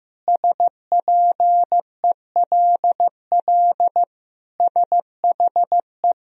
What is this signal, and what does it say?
Morse code, heard as a keyed tone, about 15 wpm, 700 Hz. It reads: SPELL SHE
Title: Sentences from Top 1000 Words - Review Part 1 - 15wpm